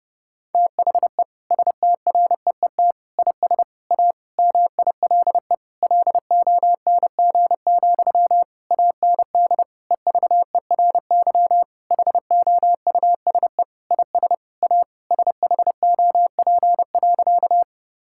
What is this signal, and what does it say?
Morse code, heard as a keyed tone, about 30 wpm, 700 Hz. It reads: THE STREET IS A MILE LONG, AND EVERY HOUSE IS A SHOP.
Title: Sentences from Top 1600 Words 30wpm